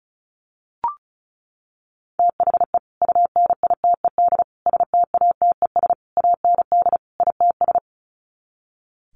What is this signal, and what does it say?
Morse code, heard as a keyed tone, about 34 wpm, 700 Hz. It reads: THE UNITED STATES AND ITS
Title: Sets of 5 Words Encore 35wpm